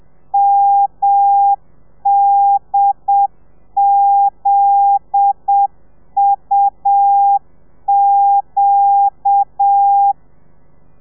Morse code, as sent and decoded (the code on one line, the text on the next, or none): -- -.. --.. ..- --.-
MDZUQ